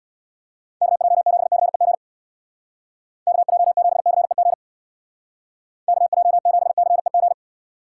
Morse code, heard as a keyed tone, about 59 wpm, 700 Hz. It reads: KJ7CEG KJ7CEG KJ7CEG
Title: Speed Racing - USA Call Signs 50wpm